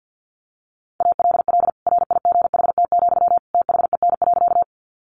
Morse code, heard as a keyed tone, about 50 wpm, 700 Hz. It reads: ALL RIGHT, THEN.
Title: Sentences from Top 200 Words 50wpm